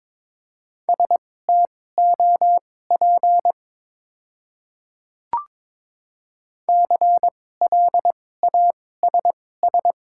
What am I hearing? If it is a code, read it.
Morse code, heard as a keyed tone, about 21 wpm, spaced as for 16 wpm, 700 Hz. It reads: STOP CLASS
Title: 2x Character Spacing - Top 500 Words - 22wpm